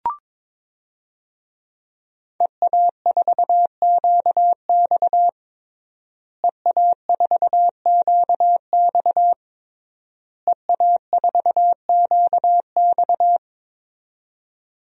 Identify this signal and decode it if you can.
Morse code, heard as a keyed tone, about 22 wpm, 700 Hz. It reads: EA4QX EA4QX EA4QX
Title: DXCC-100 Call Signs - 5 Characters - Repeated 3x - 22wpm